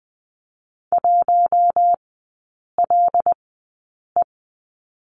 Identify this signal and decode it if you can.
Morse code, heard as a keyed tone, about 20 wpm, spaced as for 8 wpm, 700 Hz. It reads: 1LE
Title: 138 Lesson 23e - Letter f - Three Characters 20wpm